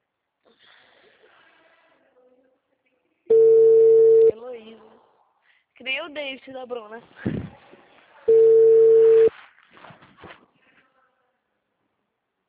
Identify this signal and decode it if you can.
Ringback tone (China, 450 Hz), with speech.